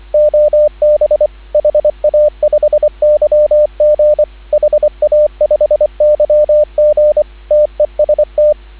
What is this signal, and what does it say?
Morse code, heard as a keyed tone, about 24 wpm, 590 Hz, with radio noise. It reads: OB HA5YG HA5YG TEST
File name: CW / 2017 HA-OB HA5YG